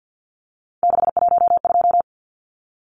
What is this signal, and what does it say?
Morse code, heard as a keyed tone, about 49 wpm, 700 Hz. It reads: 612
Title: Numbers - 3-Digit 50wpm